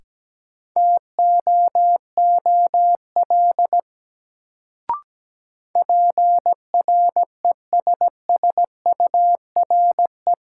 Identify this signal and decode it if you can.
Morse code, heard as a keyed tone, about 17 wpm, 700 Hz. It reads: TOOL PRESSURE
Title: Sentences from Top 1000 Words - Review Part 6 - New Word Drill - 17wpm